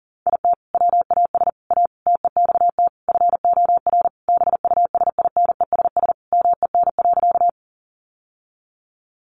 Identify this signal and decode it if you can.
Morse code, heard as a keyed tone, about 40 wpm, 700 Hz. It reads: IT WAS A TEXT FOR BUSINESS MEN.
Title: Sentences from Top 1200 Words 40wpm